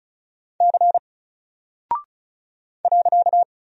Morse code, heard as a keyed tone, about 36 wpm, 700 Hz. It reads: C .
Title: Single Character (Letter:Number:Punctuation:Prosign) 35wpm